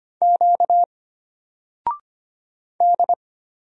Morse code, heard as a keyed tone, about 25 wpm, 700 Hz. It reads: Q D
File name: Single-Letter-Number-25wpm